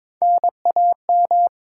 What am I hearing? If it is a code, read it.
Morse code, heard as a keyed tone, about 22 wpm, 700 Hz. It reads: N A M
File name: Top 90 QSO Elements - 8x Character Spacing - 22wpm